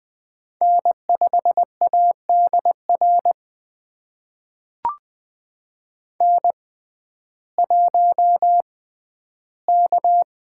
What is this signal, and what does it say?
Morse code, heard as a keyed tone, about 20 wpm, spaced as for 12 wpm, 700 Hz. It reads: N5ADR N1K
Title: USA Call Signs - 5 Characters - 6x Character Spacing - 20wpm